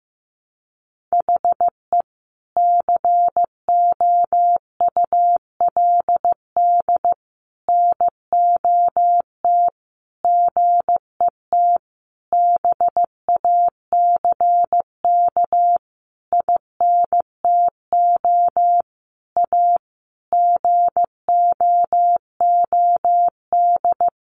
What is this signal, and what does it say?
Morse code, heard as a keyed tone, about 15 wpm, 700 Hz. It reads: HE COULD NOT GET BACK INTO A GOOD
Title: Sentences from Top 1100 Words 15wpm